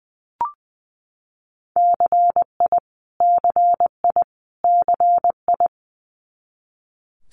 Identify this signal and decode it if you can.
Morse code, heard as a keyed tone, about 20 wpm, 700 Hz. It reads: CI CI CI
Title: Multi-letter Phonemes Repeated 3x 20wpm